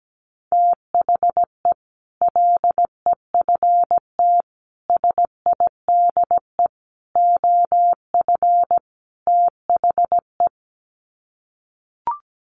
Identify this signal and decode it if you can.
Morse code, heard as a keyed tone, about 17 wpm, 700 Hz. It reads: THE LEFT SIDE OF THE
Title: Sets of 5 Words Encore 17wpm